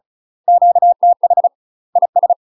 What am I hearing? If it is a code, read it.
Morse code, heard as a keyed tone, about 36 wpm, 700 Hz. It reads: QTH IS
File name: UK MASTER QSOs TRAINING / 35WPM / UK MASTER QSOs 0004